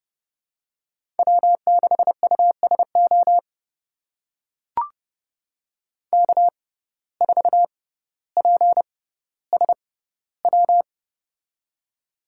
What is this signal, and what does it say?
Morse code, heard as a keyed tone, about 30 wpm, spaced as for 15 wpm, 700 Hz. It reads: W6USO K4PSW
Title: USA Call Signs - 5 Characters - 6x Character Spacing - 30wpm